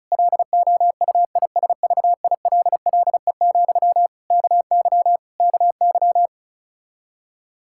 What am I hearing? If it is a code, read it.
Morse code, heard as a keyed tone, about 34 wpm, 700 Hz. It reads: LOUISVILLE, KY KY
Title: Practice QSO - Exchange 1 - Beginning - 35wpm